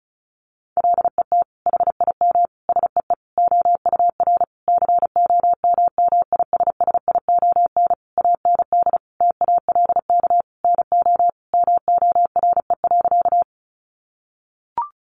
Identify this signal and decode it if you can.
Morse code, heard as a keyed tone, about 34 wpm, 700 Hz. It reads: LET HIM SEE OUR COMMISSION AND TALK NO MORE.
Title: Sentences from Top 1400 Words 35wpm